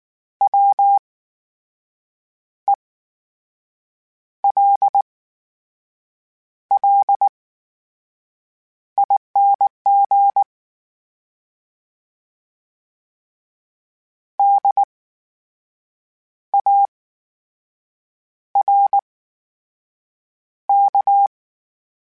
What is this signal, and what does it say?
Morse code, heard as a keyed tone, about 19 wpm, spaced as for 6 wpm, 800 Hz. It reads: WELLING DARK
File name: Special INGs / Special INGs-mp3s / VerySlowToFaster Ing @18